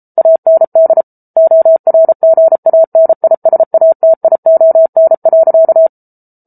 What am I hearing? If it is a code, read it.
Morse code, heard as a keyed tone, about 34 wpm, 650 Hz. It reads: AND ORGANISATION.